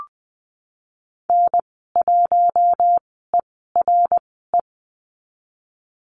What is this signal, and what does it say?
Morse code, heard as a keyed tone, about 20 wpm, spaced as for 15 wpm, 700 Hz. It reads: N1ERE
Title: 055 Lesson 10g - Letter r - Call Signs 20wpm